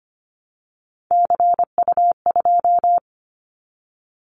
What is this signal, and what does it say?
Morse code, heard as a keyed tone, about 25 wpm, 690 Hz. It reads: CU2
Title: Callsign Prefixes for Top 50 DXCC - 25wpm